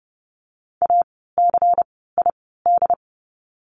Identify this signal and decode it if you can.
Morse code, heard as a keyed tone, about 30 wpm, spaced as for 17 wpm, 700 Hz. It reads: ACID
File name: Lesson 16f - Letter c - Words 30wpm